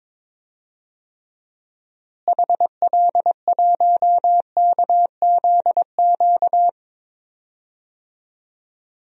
Morse code, heard as a keyed tone, about 22 wpm, 700 Hz. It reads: HL1KZQ
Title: DXCC-100 Call Signs - 6 Characters - Repeated 2x - 22wpm